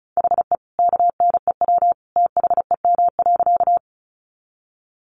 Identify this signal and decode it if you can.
Morse code, heard as a keyed tone, about 34 wpm, 700 Hz. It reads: HE KNEW THEM.
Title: Sentences from Top 400 Words 35wpm